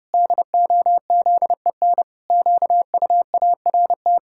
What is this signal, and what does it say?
Morse code, heard as a keyed tone, about 32 wpm, 700 Hz. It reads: DOZEN QUART
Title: Pangram Challenge v2 - 30wpm